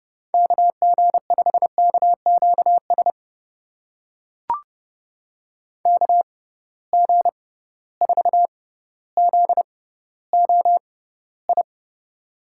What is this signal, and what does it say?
Morse code, heard as a keyed tone, about 30 wpm, spaced as for 15 wpm, 700 Hz. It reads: KG5KQS KG4ZOI
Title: USA Call Signs - 6 Characters - 6x Character Spacing - 30wpm